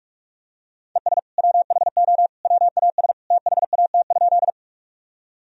Morse code, heard as a keyed tone, about 42 wpm, 700 Hz. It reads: ES WHO WAS THAT?
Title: Sentences from Top 100 Words and Abbreviations 45wpm